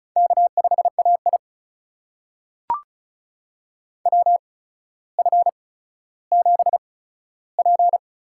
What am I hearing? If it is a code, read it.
Morse code, heard as a keyed tone, about 35 wpm, spaced as for 15 wpm, 700 Hz. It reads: K5AI WF7P
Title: USA Call Signs - 4 Characters - 8x Character Spacing - 35wpm